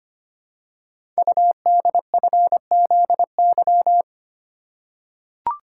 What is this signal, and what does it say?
Morse code, heard as a keyed tone, about 25 wpm, 700 Hz. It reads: UDFZY
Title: Code Groups - 5 Letters - 25wpm